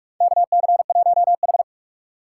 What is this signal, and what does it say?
Morse code, heard as a keyed tone, about 46 wpm, 700 Hz. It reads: KC1H
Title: USA Call Signs - 4 Characters - 4x Character Spacing - 45wpm